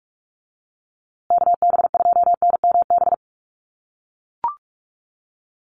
Morse code, heard as a keyed tone, about 44 wpm, 690 Hz. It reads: KB2NMB
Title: USA Call Signs Repeated 2x 45wpm